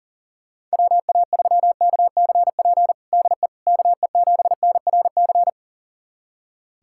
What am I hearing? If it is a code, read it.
Morse code, heard as a keyed tone, about 40 wpm, 700 Hz. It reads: WA3KCP DE KE7NRC
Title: USA Callsign DE Callsign - 6 Characters - 40wpm